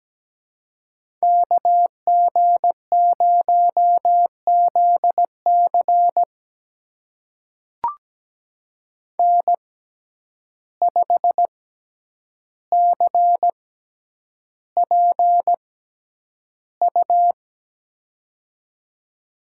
Morse code, heard as a keyed tone, about 17 wpm, spaced as for 9 wpm, 700 Hz. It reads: KG0ZC N5CPU
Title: USA Call Signs - 5 Characters - 6x Character Spacing - 17wpm